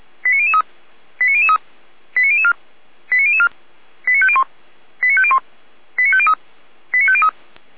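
Five-tone selective call (ZVEI), heard as quick runs of five tones, tones of about 70 ms. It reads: 89003 89003 89004 89004 89501 89501 89503 89503
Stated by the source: ZVEI five-tone call 89099 received on an SDR.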